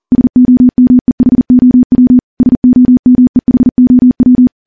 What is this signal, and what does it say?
Morse code, heard as a keyed tone, about 40 wpm, 250 Hz. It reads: SOMEHOW SOMEHOW
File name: morse / morse-training-40WPM